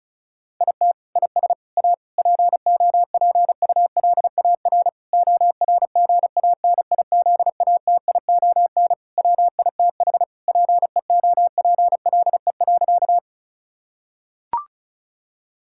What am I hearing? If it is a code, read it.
Morse code, heard as a keyed tone, about 34 wpm, 700 Hz. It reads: IT IS A POPULAR ORGANIZATION WITH PEOPLE.
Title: Sentences from Top 1000 Words - Review Part 5 - 35wpm